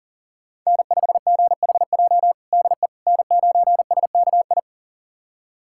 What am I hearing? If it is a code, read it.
Morse code, heard as a keyed tone, about 40 wpm, 700 Hz. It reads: N5GHJ DE N9SKI
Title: USA Callsign DE Callsign - 5 Characters - 40wpm